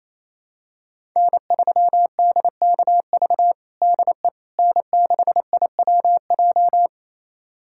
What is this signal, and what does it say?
Morse code, heard as a keyed tone, about 28 wpm, 700 Hz. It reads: N3DKV DE N6IWJ
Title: USA Callsign DE Callsign - 5 Characters - 28wpm